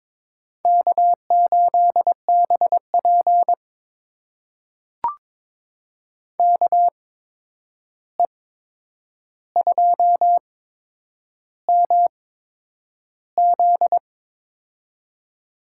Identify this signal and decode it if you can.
Morse code, heard as a keyed tone, about 22 wpm, spaced as for 9 wpm, 700 Hz. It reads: K8BP KE2MZ
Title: USA Call Signs - 8x Character Spacing - 22wpm